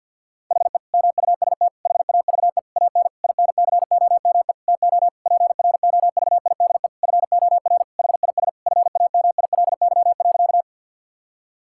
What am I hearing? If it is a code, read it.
Morse code, heard as a keyed tone, about 50 wpm, 700 Hz. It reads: HE MUST HAVE AN INCOME TO PROVIDE FOR HIS FAMILY.
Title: Sentences from Top 1000 Words - Review Part 5 - 50wpm